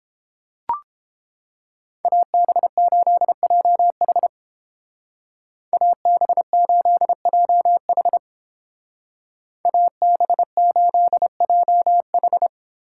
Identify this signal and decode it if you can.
Morse code, heard as a keyed tone, about 30 wpm, 700 Hz. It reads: AB8JH AB8JH AB8JH